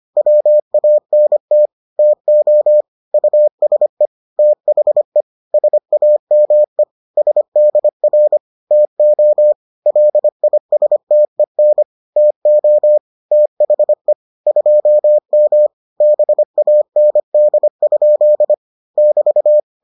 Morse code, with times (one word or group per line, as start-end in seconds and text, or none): 0.2-1.7 WANT
2.0-2.8 TO
3.1-4.1 USE
4.4-5.2 THE
5.5-6.8 SAME
7.2-8.4 SDR
8.7-9.5 TO
9.9-11.8 LISTEN
12.2-13.0 TO
13.3-14.1 THE
14.5-15.7 2M
16.0-18.5 BAND?
19.0-19.6 <BT>